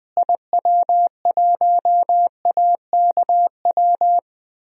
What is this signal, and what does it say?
Morse code, heard as a keyed tone, about 21 wpm, 700 Hz. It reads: IW1AKW
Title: DXCC-100 Call Signs - 6 Characters - Repeated 3x - 20wpm